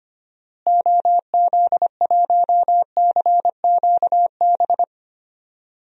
Morse code, heard as a keyed tone, about 25 wpm, 700 Hz. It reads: OZ1CQB
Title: DXCC-100 Call Signs - 6 Characters - Repeated 3x - 25wpm